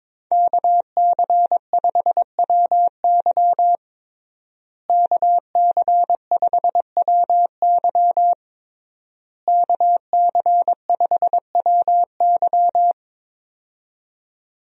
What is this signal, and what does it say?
Morse code, heard as a keyed tone, about 21 wpm, 700 Hz. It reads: KC5WY KC5WY KC5WY
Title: USA Call Signs - 5 Characters - Repeated 3x - 22wpm